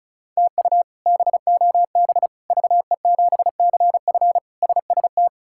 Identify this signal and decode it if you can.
Morse code, heard as a keyed tone, about 36 wpm, 700 Hz. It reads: TU BOB VE7CF SST
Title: K1USN SST QSO Practice - 35wpm